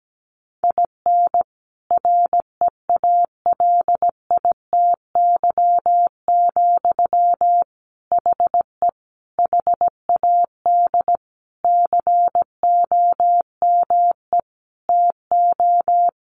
Morse code, heard as a keyed tone, about 17 wpm, 700 Hz. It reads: IN REALITY, HE HAD COME TO
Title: Sentences from Top 1500 Words 17wpm